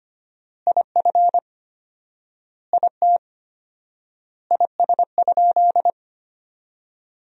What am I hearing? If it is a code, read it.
Morse code, heard as a keyed tone, about 25 wpm, 700 Hz. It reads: IF IT IS?
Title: 4x Word-Spacing - Sentences from Top 100 Words - 25wpm